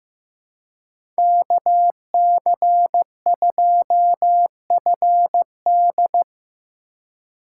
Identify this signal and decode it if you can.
Morse code, heard as a keyed tone, about 18 wpm, 700 Hz. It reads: KC2FD
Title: Speed Racing - USA 2x2 Call Signs 15wpm